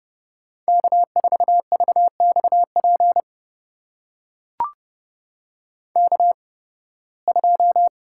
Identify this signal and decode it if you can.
Morse code, heard as a keyed tone, about 30 wpm, spaced as for 20 wpm, 700 Hz. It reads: K4VXP K2
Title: USA Call Signs - 5 Characters - 8x Character Spacing - 30wpm